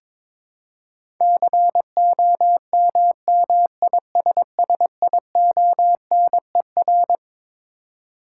Morse code, heard as a keyed tone, about 22 wpm, 700 Hz. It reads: COMMISSIONER
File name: Twelve Letter Words - 22wpm